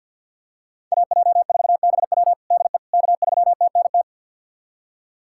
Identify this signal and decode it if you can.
Morse code, heard as a keyed tone, about 47 wpm, 700 Hz. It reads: AJ4BW DE K3TNT
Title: USA Callsign DE Callsign - 5 Characters - 50wpm